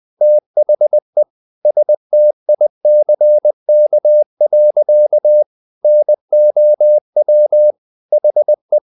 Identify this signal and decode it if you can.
Morse code, heard as a keyed tone, about 20 wpm, 600 Hz. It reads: THE STICK. NOW HE
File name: the old man and the sea / the old man and the sea part 1 20wpm